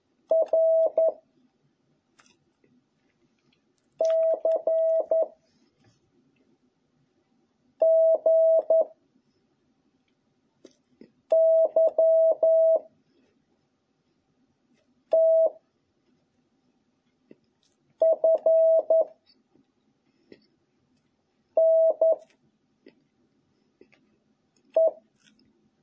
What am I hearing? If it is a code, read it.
Morse code, heard as a keyed tone, about 11 wpm, 660 Hz. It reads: R C G Y T F N E